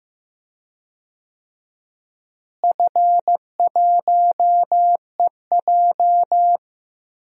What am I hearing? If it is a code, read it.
Morse code, heard as a keyed tone, about 15 wpm, 700 Hz. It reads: F1EJ
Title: DXCC-100 Call Signs - 4 Characters - Repeated 2x - 15wpm